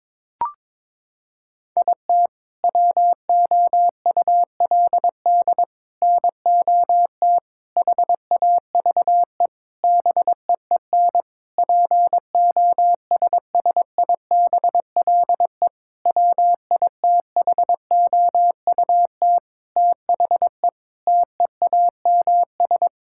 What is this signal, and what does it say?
Morse code, heard as a keyed tone, about 22 wpm, 700 Hz. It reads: IT WOULD NOT HAVE BEEN POSSIBLE WITHOUT THE TEAMS
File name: practice / 22wpm / Sentences with 7-Word Phrases - 22wpm